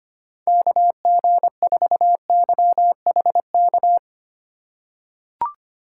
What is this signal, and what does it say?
Morse code, heard as a keyed tone, about 25 wpm, 700 Hz. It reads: KG4YHK
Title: USA Call Signs - 6 Characters - 2x Character Spacing - 25wpm